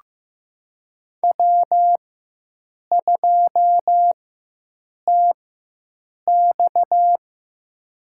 Morse code, heard as a keyed tone, about 15 wpm, spaced as for 7 wpm, 700 Hz. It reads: W2TX
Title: USA Call Signs - 4 Characters - 4x Character Spacing - 15wpm